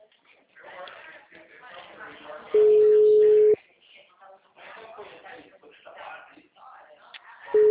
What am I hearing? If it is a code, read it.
Ringback tone (Europe, 425 Hz), with speech.
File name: out- -2009- - - .88861